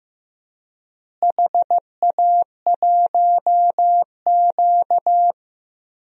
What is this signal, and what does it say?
Morse code, heard as a keyed tone, about 15 wpm, 700 Hz. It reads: HA1Q
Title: DXCC-100 Call Signs - 4 Characters - Repeated 3x - 15wpm